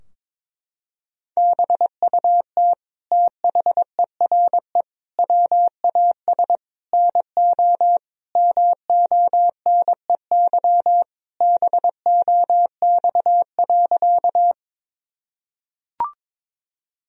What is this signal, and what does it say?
Morse code, heard as a keyed tone, about 21 wpm, 700 Hz. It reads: BUT THERE WAS NO MONEY BOX.
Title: Sentences from Top 1000 Words - Review Part 2 - 22wpm